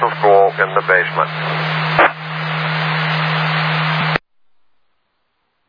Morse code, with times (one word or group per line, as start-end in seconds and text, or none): none